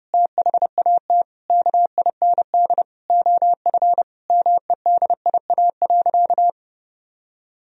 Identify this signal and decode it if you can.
Morse code, heard as a keyed tone, about 30 wpm, 700 Hz. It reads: THAT KIND OF MEDIA.
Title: Sentences from Top 1000 Words - Review Part 4 - 30wpm